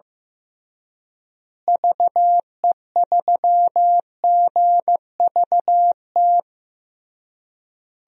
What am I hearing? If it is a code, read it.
Morse code, heard as a keyed tone, about 15 wpm, 700 Hz. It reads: VE3GVT
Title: DXCC-100 Call Signs - 6 Characters - Repeated 3x - 15wpm